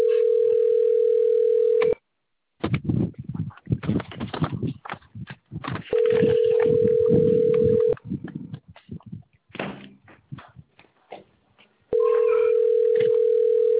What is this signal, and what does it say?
Ringback tone (North America), 440 + 480 Hz.